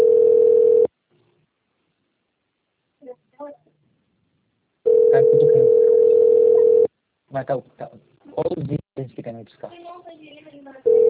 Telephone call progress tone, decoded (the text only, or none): ringback tone